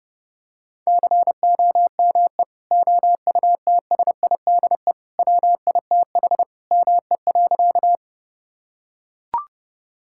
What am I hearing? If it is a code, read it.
Morse code, heard as a keyed tone, about 32 wpm, 700 Hz. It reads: COME OUTSIDE WITH ME.
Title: Sentences from Top 1000 Words - Review Part 6 - 30wpm